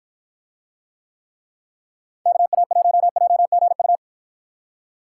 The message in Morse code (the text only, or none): KA1JGU